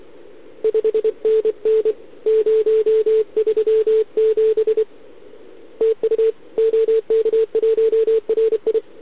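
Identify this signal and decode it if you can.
Morse code, heard as a keyed tone, about 28 wpm, 440 Hz. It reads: 5NN 037 TU OK1RI
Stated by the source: Contest CW from OK1RI on 80 m during the OKOM DX contest, recorded by a QRP station using a KX3 and a wire antenna.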